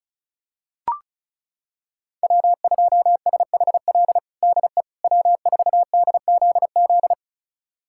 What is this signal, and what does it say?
Morse code, heard as a keyed tone, about 34 wpm, 700 Hz. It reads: W2SHL DE W4DZZ
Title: USA Callsign DE Callsign - 5 Characters - 35wpm